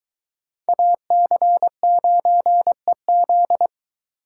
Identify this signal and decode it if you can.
Morse code, heard as a keyed tone, about 23 wpm, 700 Hz. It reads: AC9EZ